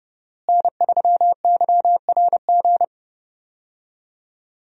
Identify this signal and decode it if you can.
Morse code, heard as a keyed tone, about 32 wpm, 700 Hz. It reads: N3YRG
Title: USA Call Signs - 5 Characters - Repeated 3x - 30wpm